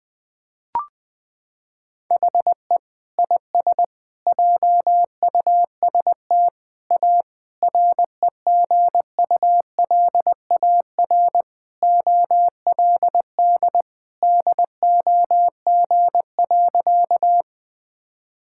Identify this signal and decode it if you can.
Morse code, heard as a keyed tone, about 20 wpm, 700 Hz. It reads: HE IS JUST A REGULAR OLD DOG.
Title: Sentences from Top 1000 Words - Review Part 6 - 20wpm